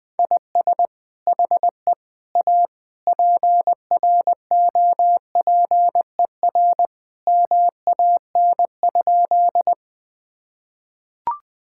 Morse code, heard as a keyed tone, about 21 wpm, 700 Hz. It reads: IS HE A PROPER MAN?
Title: Sentences from Top 1000 Words - Review Part 5 - 20wpm